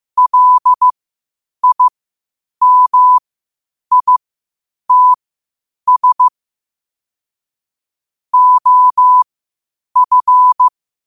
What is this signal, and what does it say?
Morse code, heard as a keyed tone, about 15 wpm, spaced as for 8 wpm, 1000 Hz. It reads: LIMITS OF
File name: qotdfiles / 8wpm